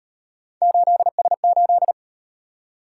Morse code, heard as a keyed tone, about 38 wpm, 700 Hz. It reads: 8S8